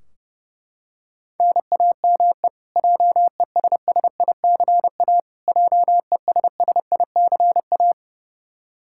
Morse code, heard as a keyed tone, about 30 wpm, 700 Hz. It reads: NAME JESSICA JESSICA
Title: Practice QSO - Exchange 1 - Beginning - 30wpm